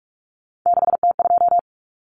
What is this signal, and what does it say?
Morse code, heard as a keyed tone, about 44 wpm, 700 Hz. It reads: 6T2
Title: Common Cut Numbers - 3-Digit 45wpm